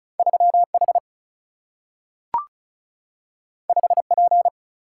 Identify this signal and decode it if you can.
Morse code, heard as a keyed tone, about 36 wpm, 700 Hz. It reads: 3 H 5 P
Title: Single-Letter Single-Number 35wpm